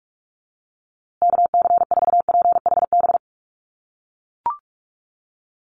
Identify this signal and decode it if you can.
Morse code, heard as a keyed tone, about 44 wpm, 700 Hz. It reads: KC4PHB
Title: USA Call Signs Repeated 2x 45wpm